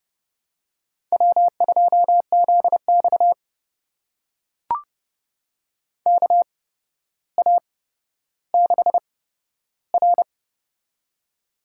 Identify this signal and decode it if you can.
Morse code, heard as a keyed tone, about 30 wpm, spaced as for 13 wpm, 700 Hz. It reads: W2ZX KA6R
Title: USA Call Signs - 4 Characters - 8x Character Spacing - 30wpm